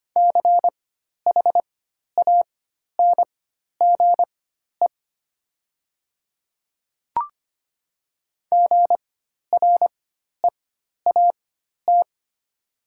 Morse code, heard as a keyed tone, about 25 wpm, spaced as for 12 wpm, 700 Hz. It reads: CHANGE GREAT
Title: 4x Character Spacing - Top 200 Words - 25wpm